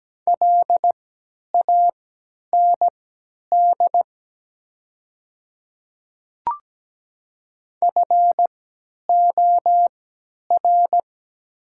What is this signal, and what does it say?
Morse code, heard as a keyed tone, about 17 wpm, spaced as for 10 wpm, 700 Hz. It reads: LAND FOR
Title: 3x Character Spacing - Top 500 Words - 17wpm